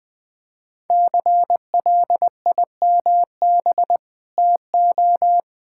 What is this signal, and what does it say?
Morse code, heard as a keyed tone, about 20 wpm, 700 Hz. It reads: CLIMB TO
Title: Sentences with 5-Word Phrases - 20wpm